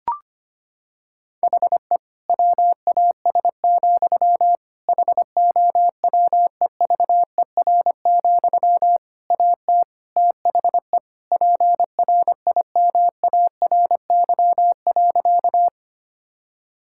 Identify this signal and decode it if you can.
Morse code, heard as a keyed tone, about 25 wpm, 700 Hz. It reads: HE WAS, HOWEVER, AT THE PRIMARY.
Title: Sentences from Top 1000 Words - Review Part 5 - 25wpm